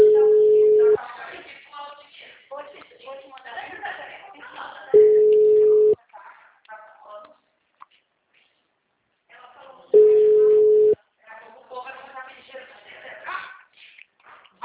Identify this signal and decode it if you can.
Ringback tone (Europe, 425 Hz), with speech.